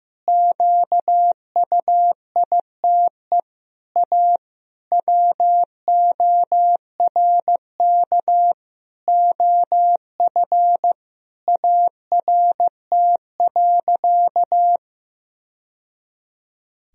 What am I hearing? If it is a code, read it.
Morse code, heard as a keyed tone, about 15 wpm, 700 Hz. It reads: QUITE A WORK OF ART.
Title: Sentences from Top 1000 Words - Review Part 3 - 15wpm